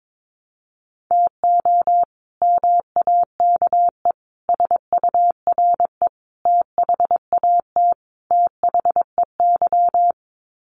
Morse code, heard as a keyed tone, about 22 wpm, 700 Hz. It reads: TO MAKE SURE THAT THEY
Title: Sets of 5 Words Encore2 22wpm